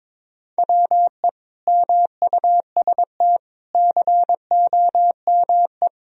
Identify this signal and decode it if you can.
Morse code, heard as a keyed tone, about 22 wpm, 700 Hz. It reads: WE MUST COME
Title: Sentences from Top 1000 Words - Review Part 1 - 22wpm